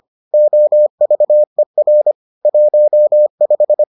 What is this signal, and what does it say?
Morse code, heard as a keyed tone, about 25 wpm, 600 Hz. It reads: OVER 15